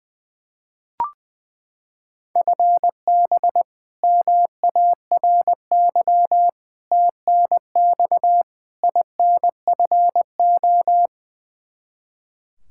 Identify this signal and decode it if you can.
Morse code, heard as a keyed tone, about 21 wpm, 700 Hz. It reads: FB MARY TNX INFO
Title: Practice QSO - Exchange 3 - More Info - 20wpm